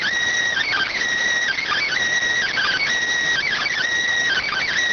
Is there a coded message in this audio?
Yes: Emergency Alert System (SAME) tones from 0.6 to 4.7 s, heard as harsh buzzing bursts.